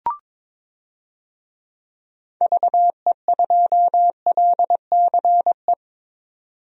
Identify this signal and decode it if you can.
Morse code, heard as a keyed tone, about 22 wpm, 700 Hz. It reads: VE2LCE
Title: DXCC-100 Call Signs - 6 Characters - 22wpm